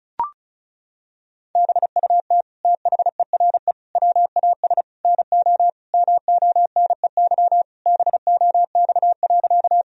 Morse code, heard as a keyed tone, about 34 wpm, 700 Hz. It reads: BUT THERE WAS NO MONEY BOX.
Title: Sentences from Top 1000 Words - Review Part 2 - 35wpm